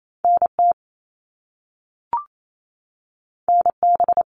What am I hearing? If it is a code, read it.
Morse code, heard as a keyed tone, about 28 wpm, 700 Hz. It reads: NT NB
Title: Canadian Provinces and Territories 28wpm